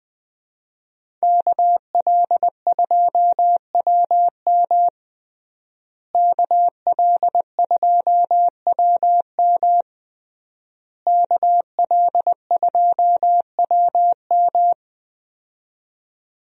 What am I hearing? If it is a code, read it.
Morse code, heard as a keyed tone, about 20 wpm, 700 Hz. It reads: KL2WM KL2WM KL2WM
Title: USA Call Signs - 5 Characters - Repeated 3x - 20wpm